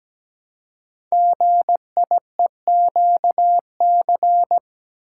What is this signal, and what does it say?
Morse code, heard as a keyed tone, about 17 wpm, 690 Hz. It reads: GIEQC
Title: Code Groups - 5 Letter:Numbers - 17wpm